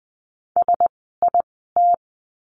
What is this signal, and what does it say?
Morse code, heard as a keyed tone, about 20 wpm, spaced as for 14 wpm, 700 Hz. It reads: SIT
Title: 094 Lesson 16f - Letter c - Words 20wpm